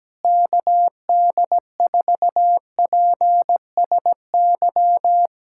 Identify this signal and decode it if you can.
Morse code, heard as a keyed tone, about 17 wpm, 700 Hz. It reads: KD4PSY
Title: USA Callsign DE Callsign 17wpm